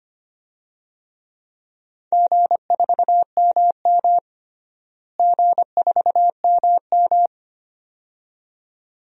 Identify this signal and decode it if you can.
Morse code, heard as a keyed tone, about 25 wpm, 700 Hz. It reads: G4MM G4MM
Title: DXCC-100 Call Signs - 4 Characters - Repeated 2x - 25wpm